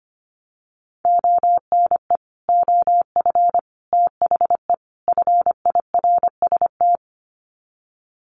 Morse code, heard as a keyed tone, about 25 wpm, 700 Hz. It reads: ONE OF THE FIRST
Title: Sets of 4 Words 25wpm